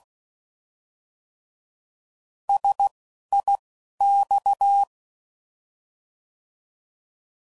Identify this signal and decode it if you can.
Morse code, heard as a keyed tone, about 16 wpm, spaced as for 12 wpm, 800 Hz. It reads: SIX